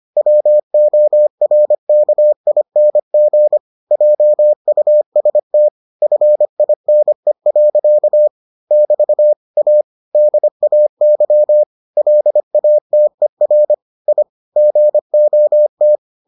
Morse code, heard as a keyed tone, about 25 wpm, 600 Hz. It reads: WORKING JUST FINE. <BT> A DAY LATER I GOT